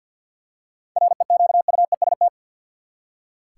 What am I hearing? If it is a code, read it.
Morse code, heard as a keyed tone, about 46 wpm, 700 Hz. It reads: REQUEST
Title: Seven Letter Words - 50wpm